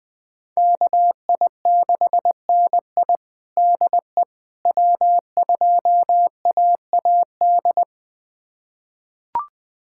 Morse code, heard as a keyed tone, about 20 wpm, 700 Hz. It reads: KI6NI DE W2AAD
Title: USA Callsign DE Callsign - 5 Characters - 20wpm